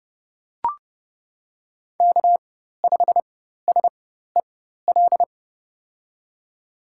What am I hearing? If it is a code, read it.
Morse code, heard as a keyed tone, about 30 wpm, spaced as for 14 wpm, 700 Hz. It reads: K5SEL
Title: USA Call Signs - 5 Characters - 4x Character Spacing - 30wpm